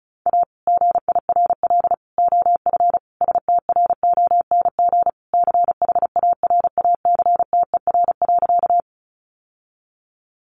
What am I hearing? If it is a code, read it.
Morse code, heard as a keyed tone, about 34 wpm, 700 Hz. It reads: A GIRL OF STRONG CHARACTER.
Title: Sentences from Top 1200 Words 35wpm